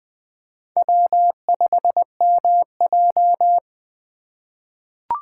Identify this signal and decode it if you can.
Morse code, heard as a keyed tone, about 20 wpm, 700 Hz. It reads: W5MJ
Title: USA Call Signs - 4 Characters - 2x Character Spacing - 20wpm